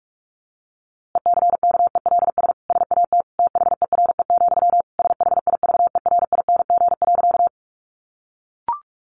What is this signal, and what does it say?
Morse code, heard as a keyed tone, about 44 wpm, 700 Hz. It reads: ECKELS SAT THERE, SHIVERING.
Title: A Sound Of Thunder 45wpm